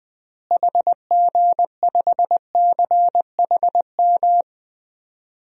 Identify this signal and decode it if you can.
Morse code, heard as a keyed tone, about 20 wpm, 700 Hz. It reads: HG5CHM
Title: DXCC-100 Call Signs - 6 Characters - Repeated 3x - 20wpm